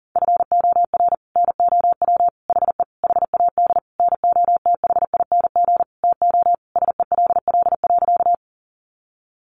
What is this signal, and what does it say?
Morse code, heard as a keyed tone, about 37 wpm, 700 Hz. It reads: FOR NOW HE HAD NOTHING TO SELL.
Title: Sentences from Top 1100 Words 40wpm